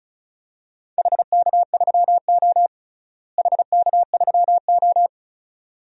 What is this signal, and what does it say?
Morse code, heard as a keyed tone, about 34 wpm, 700 Hz. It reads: HK3O HK3O
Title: DXCC-100 Call Signs - 4 Characters - Repeated 2x - 35wpm